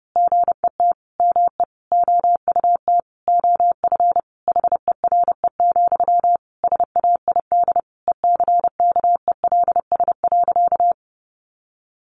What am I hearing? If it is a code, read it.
Morse code, heard as a keyed tone, about 30 wpm, 700 Hz. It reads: GET ME OUT OF HERE, SAID ECKELS.
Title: A Sound Of Thunder 30wpm